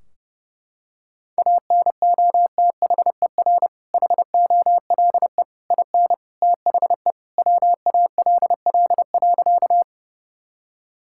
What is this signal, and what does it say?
Morse code, heard as a keyed tone, about 30 wpm, 700 Hz. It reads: ANOTHER HOLE IN THE WALL.
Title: Sentences from Top 1600 Words 30wpm